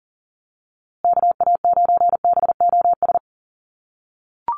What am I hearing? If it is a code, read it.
Morse code, heard as a keyed tone, about 40 wpm, 700 Hz. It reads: KA9BOS
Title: USA Call Signs Repeated 3x 40wpm